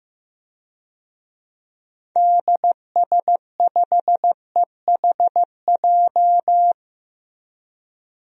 Morse code, heard as a keyed tone, about 15 wpm, 700 Hz. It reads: DS5EHJ
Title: DXCC-100 Call Signs - 6 Characters - Repeated 2x - 15wpm